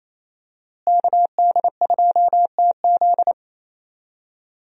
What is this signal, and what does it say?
Morse code, heard as a keyed tone, about 28 wpm, 700 Hz. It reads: KD2TZ
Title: USA Call Signs - 5 Characters - Repeated 3x - 28wpm